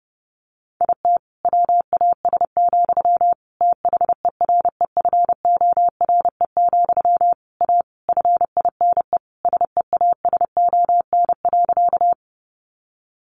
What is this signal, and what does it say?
Morse code, heard as a keyed tone, about 30 wpm, 700 Hz. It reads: IT WAS, THEREFORE, A FINE SEASON.
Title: Sentences from Top 1400 Words 30wpm